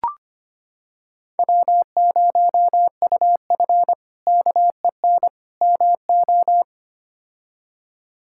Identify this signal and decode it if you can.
Morse code, heard as a keyed tone, about 25 wpm, 700 Hz. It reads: W0UF KEN MO
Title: K1USN SST QSO Practice - 25wpm